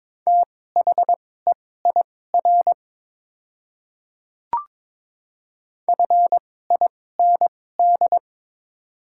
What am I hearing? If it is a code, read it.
Morse code, heard as a keyed tone, about 22 wpm, spaced as for 16 wpm, 700 Hz. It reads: THEIR FIND
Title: 2x Character Spacing - Top 100 Words - 22wpm